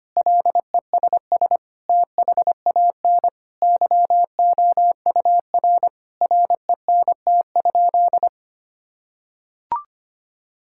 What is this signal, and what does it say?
Morse code, heard as a keyed tone, about 25 wpm, 700 Hz. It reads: LESS THAN YOUR RENT?
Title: Sentences from Top 1000 Words - Review Part 5 - 25wpm